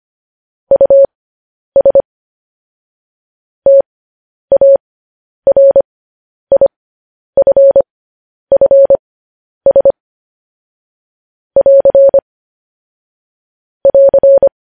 Morse code, heard as a keyed tone, about 25 wpm, spaced as for 10 wpm, 550 Hz. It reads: US TARIFFS <AR> <AR>
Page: news headlines in morse code at 10 wpm